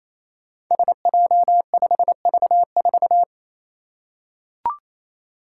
Morse code, heard as a keyed tone, about 28 wpm, 700 Hz. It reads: SJ5V4
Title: Five Challenging Letter-Number Combinations 28wpm